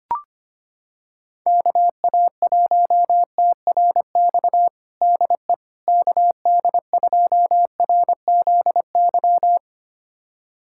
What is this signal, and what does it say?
Morse code, heard as a keyed tone, about 25 wpm, 700 Hz. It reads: KA1TRX DE KD2RZY
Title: USA Callsign DE Callsign - 6 Characters - 25wpm